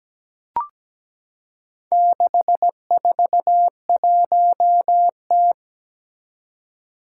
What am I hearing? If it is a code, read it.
Morse code, heard as a keyed tone, about 17 wpm, 700 Hz. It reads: 641T
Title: Four Challenging Letter-Number Combinations 17wpm